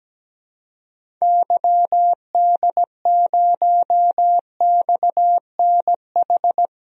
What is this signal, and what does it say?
Morse code, heard as a keyed tone, about 17 wpm, 700 Hz. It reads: YD0XNH
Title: DXCC-100 Call Signs - 6 Characters - 17wpm